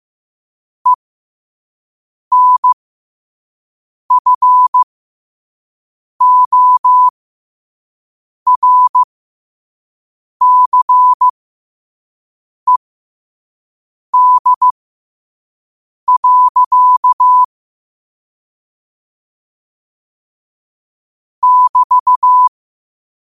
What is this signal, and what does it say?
Morse code, heard as a keyed tone, about 15 wpm, spaced as for 5 wpm, 1000 Hz. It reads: ENFORCED. <BT>
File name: qotdfiles / 5wpm